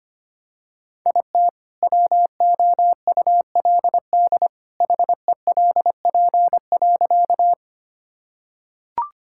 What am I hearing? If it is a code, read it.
Morse code, heard as a keyed tone, about 25 wpm, 700 Hz. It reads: IT WOULD HELP.
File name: practice / 25wpm / Sentences with 2-Word Phrases - 25wpm